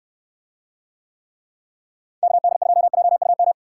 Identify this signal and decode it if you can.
Morse code, heard as a keyed tone, about 64 wpm, 700 Hz. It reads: KD2JUM